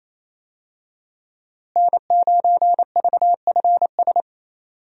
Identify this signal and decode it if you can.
Morse code, heard as a keyed tone, about 28 wpm, 700 Hz. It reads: N9VFS
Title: USA Call Signs - 5 Characters - Repeated 2x - 28wpm